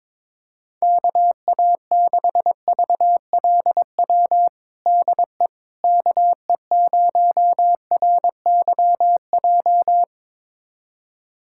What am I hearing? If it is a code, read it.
Morse code, heard as a keyed tone, about 21 wpm, 700 Hz. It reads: KA6VLW DE KE0RYJ
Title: USA Callsign DE Callsign - 6 Characters - 22wpm